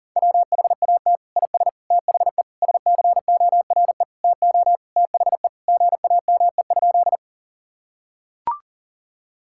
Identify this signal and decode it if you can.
Morse code, heard as a keyed tone, about 37 wpm, 700 Hz. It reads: WHAT IS THE SCORE TO THE GAME?
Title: Sentences from Top 1600 Words 40wpm